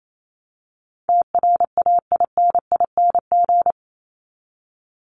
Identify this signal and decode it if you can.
Morse code, heard as a keyed tone, about 28 wpm, 700 Hz. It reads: TRAINING
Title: Top 501-750 Words 28wpm